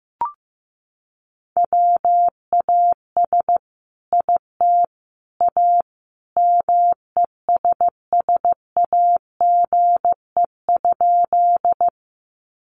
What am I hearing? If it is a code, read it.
Morse code, heard as a keyed tone, about 15 wpm, 700 Hz. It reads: WAS IT A MESSAGE?
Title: Sentences from Top 1100 Words 15wpm